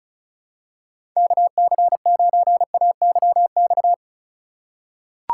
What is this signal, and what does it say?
Morse code, heard as a keyed tone, about 36 wpm, 700 Hz. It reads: KC9AYX
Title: USA Call Signs - 6 Characters - 3x Character Spacing - 35wpm